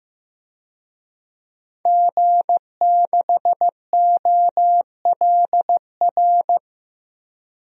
Morse code, heard as a keyed tone, about 15 wpm, 700 Hz. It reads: G6OLR
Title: DXCC-100 Call Signs - 5 Characters - 15wpm